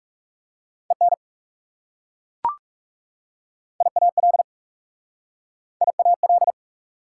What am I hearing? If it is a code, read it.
Morse code, heard as a keyed tone, about 40 wpm, 690 Hz. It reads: EN IAL IAL